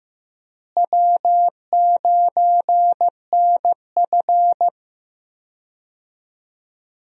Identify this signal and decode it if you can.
Morse code, heard as a keyed tone, about 15 wpm, 700 Hz. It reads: W9NF
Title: USA Call Signs - 4 Characters - Repeated 2x - 15wpm